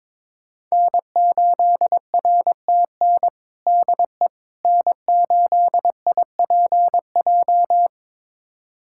Morse code, heard as a keyed tone, about 22 wpm, 700 Hz. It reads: N8RTN DE N8IPJ
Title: USA Callsign DE Callsign - 5 Characters - 22wpm